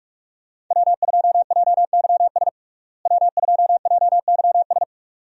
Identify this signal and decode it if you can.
Morse code, heard as a keyed tone, about 47 wpm, 700 Hz. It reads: W2JYS W2JYS
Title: USA Call Signs - 5 Characters - Repeated 2x - 45wpm